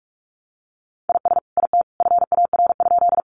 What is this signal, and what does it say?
Morse code, heard as a keyed tone, about 44 wpm, 700 Hz. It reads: IS IT FAR?
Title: Sentences from Top 300 Words 45wpm